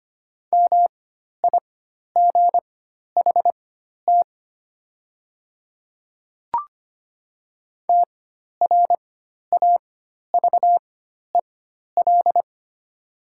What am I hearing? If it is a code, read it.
Morse code, heard as a keyed tone, about 25 wpm, spaced as for 12 wpm, 690 Hz. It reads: MIGHT TRAVEL
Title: 4x Character Spacing - Top 500 Words - 25wpm